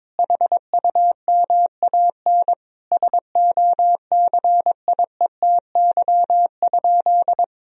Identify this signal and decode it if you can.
Morse code, heard as a keyed tone, about 22 wpm, 700 Hz. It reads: HUMAN SOCIETY?
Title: Sentences from Top 1000 Words - Review Part 4 - 22wpm